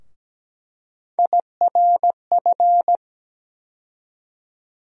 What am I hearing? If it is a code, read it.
Morse code, heard as a keyed tone, about 17 wpm, 700 Hz. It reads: IRF
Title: USA Call Sign Postfixes - 3 Letters 17wpm